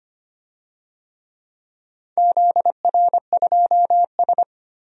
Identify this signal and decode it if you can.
Morse code, heard as a keyed tone, about 25 wpm, 700 Hz. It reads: ZR2S
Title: DXCC-100 Call Signs - 4 Characters - 25wpm